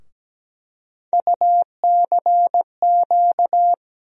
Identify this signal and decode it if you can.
Morse code, heard as a keyed tone, about 17 wpm, 700 Hz. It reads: UCQ
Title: USA Call Sign Postfixes - 3 Letters 17wpm